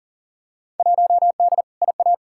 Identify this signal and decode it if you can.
Morse code, heard as a keyed tone, about 40 wpm, 700 Hz. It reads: 1D IA
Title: Field Day Exchanges - 40wpm